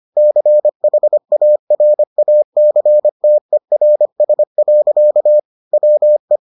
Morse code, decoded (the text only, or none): CHARACTERS. WE